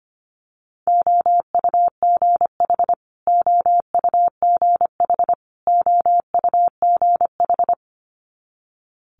Morse code, heard as a keyed tone, about 25 wpm, 700 Hz. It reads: OUGH OUGH OUGH
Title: Multi-letter Phonemes Repeated 3x 25wpm